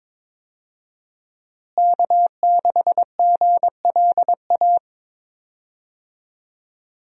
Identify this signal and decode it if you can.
Morse code, heard as a keyed tone, about 22 wpm, 700 Hz. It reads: K6GLA
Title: USA Call Signs - 5 Characters - Repeated 3x - 22wpm